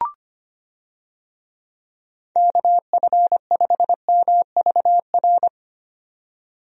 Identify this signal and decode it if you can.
Morse code, heard as a keyed tone, about 25 wpm, 700 Hz. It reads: KF5MVR
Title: USA Call Signs - 6 Characters - 25wpm